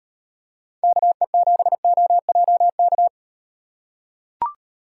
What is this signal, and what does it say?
Morse code, heard as a keyed tone, about 37 wpm, 700 Hz. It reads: KE7OJK